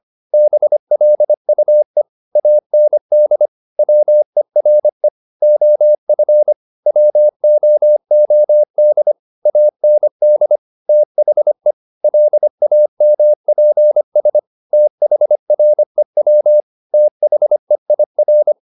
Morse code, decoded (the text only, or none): BLUE AND WERE OF WOOD AND THE LAMPS THREW THEIR